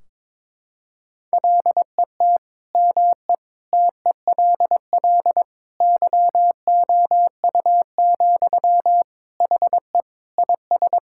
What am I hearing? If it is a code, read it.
Morse code, heard as a keyed tone, about 22 wpm, 700 Hz. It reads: LET ME TELL YOU, HE IS
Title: Sentences from Top 1600 Words 22wpm